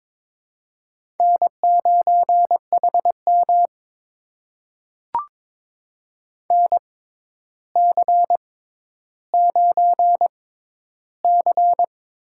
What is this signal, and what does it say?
Morse code, heard as a keyed tone, about 22 wpm, spaced as for 11 wpm, 700 Hz. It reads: N9HM NC9C
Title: USA Call Signs - 6x Character Spacing - 22wpm